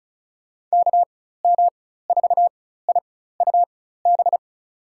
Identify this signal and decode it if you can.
Morse code, heard as a keyed tone, about 36 wpm, spaced as for 16 wpm, 700 Hz. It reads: KM4IUB
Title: USA Call Signs - 6 Characters - 4x Character Spacing - 35wpm